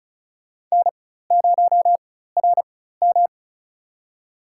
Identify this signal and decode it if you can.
Morse code, heard as a keyed tone, about 34 wpm, spaced as for 16 wpm, 700 Hz. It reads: N0RM
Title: USA Call Signs - 4x Character Spacing - 35wpm